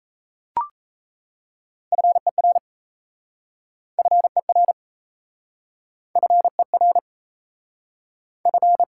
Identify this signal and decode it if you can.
Morse code, heard as a keyed tone, about 37 wpm, 700 Hz. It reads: FER FER FER F